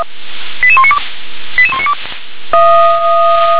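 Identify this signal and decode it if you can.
Five-tone selective call (ZVEI), heard as quick runs of five tones, tones of about 70 ms. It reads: 88192 88192
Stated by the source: SDR-received ZVEI five-tone selective call.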